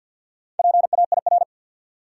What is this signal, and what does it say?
Morse code, heard as a keyed tone, about 51 wpm, 700 Hz. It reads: PAIR
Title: Sets of 3 Words - New Word Drill - 50wpm